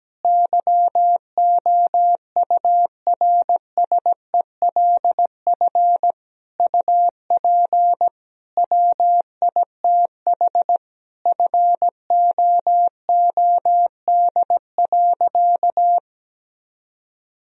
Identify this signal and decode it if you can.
Morse code, heard as a keyed tone, about 17 wpm, 700 Hz. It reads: YOURSELF UP WITH FOOD.
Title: Sentences from Top 1000 Words - Review Part 3 - 17wpm